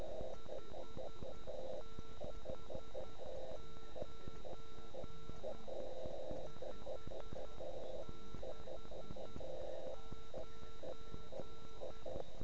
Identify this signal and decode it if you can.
Morse code, heard as a keyed tone, about 10 wpm, 1340 Hz. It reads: 569 569